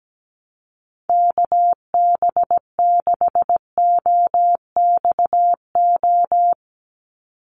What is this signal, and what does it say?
Morse code, heard as a keyed tone, about 17 wpm, 700 Hz. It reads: KB6OXO
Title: USA Call Signs 17wpm